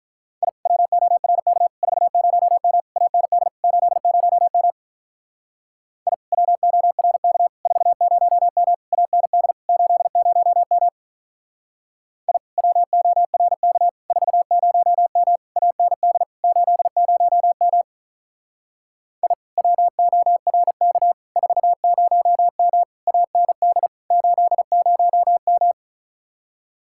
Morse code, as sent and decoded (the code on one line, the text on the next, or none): .. / .-- --- .-. -.- / ....- ----- -- / .- -. -.. / ---.. ----- -- / .. / .-- --- .-. -.- / ....- ----- -- / .- -. -.. / ---.. ----- -- / .. / .-- --- .-. -.- / ....- ----- -- / .- -. -.. / ---.. ----- -- / .. / .-- --- .-. -.- / ....- ----- -- / .- -. -.. / ---.. ----- --
I WORK 40M AND 80M I WORK 40M AND 80M I WORK 40M AND 80M I WORK 40M AND 80M